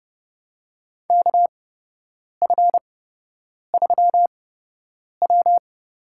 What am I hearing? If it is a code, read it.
Morse code, heard as a keyed tone, about 29 wpm, spaced as for 8 wpm, 700 Hz. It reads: KF3W
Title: USA Call Signs - 4 Characters - 8x Character Spacing - 30wpm